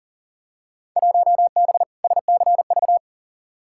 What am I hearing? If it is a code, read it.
Morse code, heard as a keyed tone, about 40 wpm, 700 Hz. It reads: 1B SCV
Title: Field Day Exchanges - 40wpm